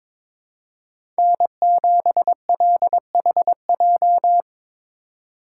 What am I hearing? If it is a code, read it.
Morse code, heard as a keyed tone, about 22 wpm, 700 Hz. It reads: N7LHJ
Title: USA Call Signs - 4x Character Spacing - 22wpm